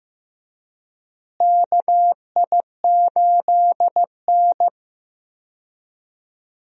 Morse code, heard as a keyed tone, about 15 wpm, 700 Hz. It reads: KI8N
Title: USA Call Signs - 4 Characters - Repeated 2x - 15wpm